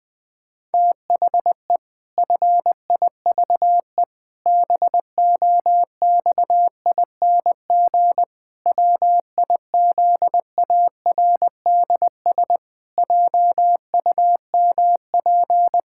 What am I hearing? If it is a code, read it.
Morse code, heard as a keyed tone, about 20 wpm, 700 Hz. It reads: THE FIVE BOXING WIZARDS JUMP
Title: Pangram Challenge v2 - 20wpm